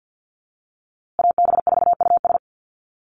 Morse code, heard as a keyed tone, about 50 wpm, 700 Hz. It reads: AB4US
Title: USA Call Signs Repeated 2x 50wpm